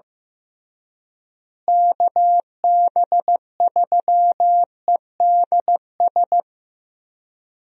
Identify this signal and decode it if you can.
Morse code, heard as a keyed tone, about 16 wpm, 700 Hz. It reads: KB3EDS
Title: USA Call Signs - 6 Characters - Repeated 3x - 15wpm